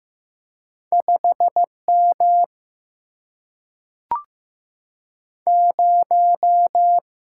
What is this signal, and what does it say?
Morse code, heard as a keyed tone, about 15 wpm, 700 Hz. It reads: 5 M 0
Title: Single-Letter Single-Number 15wpm